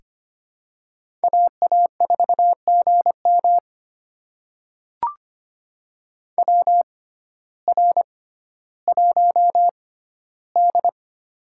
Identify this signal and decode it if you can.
Morse code, heard as a keyed tone, about 25 wpm, spaced as for 14 wpm, 700 Hz. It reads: AA4GM WR1D
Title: USA Call Signs - 6x Character Spacing - 25wpm